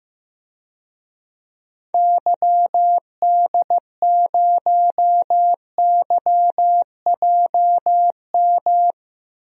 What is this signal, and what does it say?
Morse code, heard as a keyed tone, about 15 wpm, 700 Hz. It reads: YD0YJM
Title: DXCC-100 Call Signs - 6 Characters - Repeated 3x - 15wpm